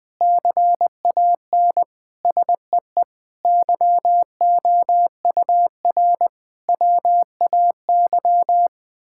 Morse code, heard as a keyed tone, about 20 wpm, 700 Hz. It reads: CAN SEE YOUR WAY
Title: Sentences with 9-Word Phrases - Spoken Beginning - 20wpm